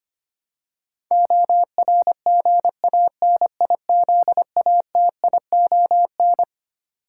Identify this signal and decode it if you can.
Morse code, heard as a keyed tone, about 25 wpm, 700 Hz. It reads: ORGANIZATION
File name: practice / 25wpm / Twelve Letter Words - 25wpm